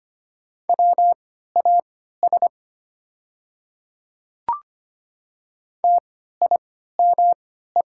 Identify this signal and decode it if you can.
Morse code, heard as a keyed tone, about 25 wpm, spaced as for 14 wpm, 700 Hz. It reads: WAS TIME
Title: 3x Character Spacing - Top 100 Words - 25wpm